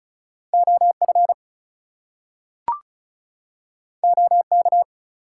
Morse code, heard as a keyed tone, about 36 wpm, 700 Hz. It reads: OF OK
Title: Two Letter Words - 35wpm